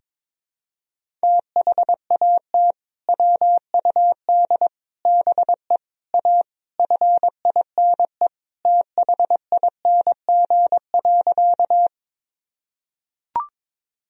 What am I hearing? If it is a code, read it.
Morse code, heard as a keyed tone, about 22 wpm, 700 Hz. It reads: THAT WUD BE A FINE THING.
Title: Sentences from Top 500 Words and Abbreviations 22wpm